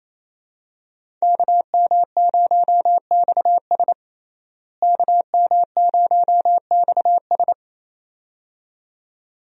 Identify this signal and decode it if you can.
Morse code, heard as a keyed tone, about 28 wpm, 700 Hz. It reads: KM0XS KM0XS
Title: USA Call Signs - 5 Characters - Repeated 2x - 28wpm